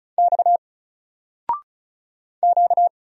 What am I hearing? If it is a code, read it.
Morse code, heard as a keyed tone, about 35 wpm, 700 Hz. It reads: X Q
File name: Single-Letter-Number-35wpm